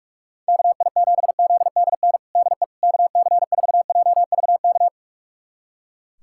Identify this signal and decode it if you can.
Morse code, heard as a keyed tone, about 43 wpm, 700 Hz. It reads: KI7ZDN DE KC4JVK
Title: USA Callsign DE Callsign - 6 Characters - 45wpm